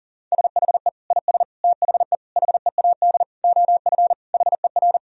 Morse code, heard as a keyed tone, about 40 wpm, 700 Hz. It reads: SHE IS THE HEAD OF HER
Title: Sentences from Top 1000 Words - Review Part 2 - 40wpm